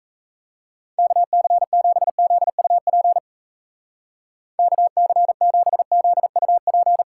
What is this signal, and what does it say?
Morse code, heard as a keyed tone, about 37 wpm, 700 Hz. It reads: KC7ZUP KC7ZUP